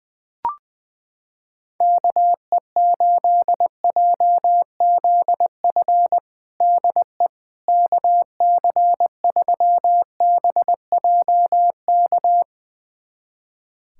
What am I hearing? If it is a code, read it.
Morse code, heard as a keyed tone, about 20 wpm, 700 Hz. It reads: KE8JZF DE KC3BJK
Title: USA Callsign DE Callsign - 6 Characters - 20wpm